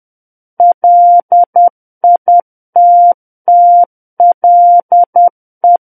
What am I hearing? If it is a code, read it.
Morse code, heard as a keyed tone, about 10 wpm, 700 Hz. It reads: LITTLE
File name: Book One / 01-Chapter 1 - 10 WPM